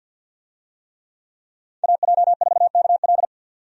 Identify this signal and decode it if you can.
Morse code, heard as a keyed tone, about 50 wpm, 700 Hz. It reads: AJ4KL
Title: USA Call Signs - 5 Characters - 50wpm